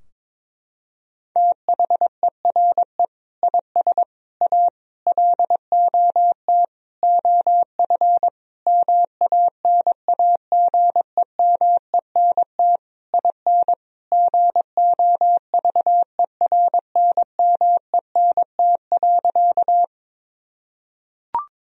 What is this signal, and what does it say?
Morse code, heard as a keyed tone, about 22 wpm, 700 Hz. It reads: THERE IS A LOT OF MANAGEMENT IN GOVERNMENT.
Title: Sentences from Top 1000 Words - Review Part 4 - 22wpm